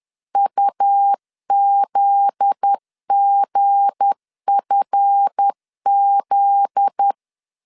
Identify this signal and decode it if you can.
Morse code, heard as a keyed tone, about 10 wpm, 800 Hz. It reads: UZGFZ
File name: LEZIONI / 7 GF9Z / 7 GF9Z 10wpm 2611